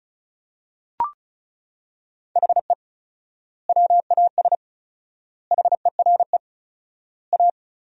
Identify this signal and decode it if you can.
Morse code, heard as a keyed tone, about 36 wpm, 700 Hz. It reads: HE WAS HERE A
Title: 4x Word-Spacing - Sentences from Top 500 Words - 35wpm